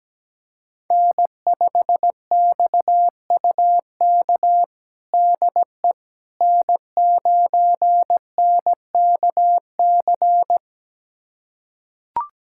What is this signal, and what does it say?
Morse code, heard as a keyed tone, about 17 wpm, 700 Hz. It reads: N5XUK DE N9NKC
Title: USA Callsign DE Callsign - 5 Characters - 17wpm